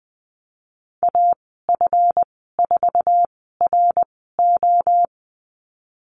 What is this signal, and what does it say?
Morse code, heard as a keyed tone, about 20 wpm, spaced as for 15 wpm, 700 Hz. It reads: AF4RO
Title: 154 Lesson 25g - Letter p - Call Signs 20wpm